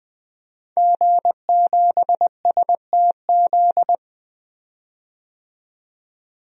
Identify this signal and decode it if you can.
Morse code, heard as a keyed tone, about 20 wpm, 700 Hz. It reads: G7STZ
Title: DXCC-100 Call Signs - 5 Characters - 20wpm